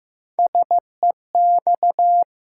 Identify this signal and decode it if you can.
Morse code, heard as a keyed tone, about 15 wpm, 700 Hz. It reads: SEX
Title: Three Letter Words - 15wpm